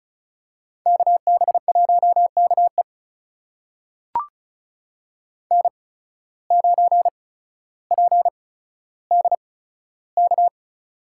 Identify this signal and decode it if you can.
Morse code, heard as a keyed tone, about 36 wpm, spaced as for 15 wpm, 700 Hz. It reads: KB1KE N9PDK
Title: USA Call Signs - 5 Characters - 8x Character Spacing - 35wpm